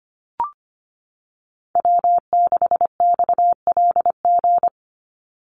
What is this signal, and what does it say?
Morse code, heard as a keyed tone, about 25 wpm, 700 Hz. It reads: W6XLG
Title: USA Call Signs 25wpm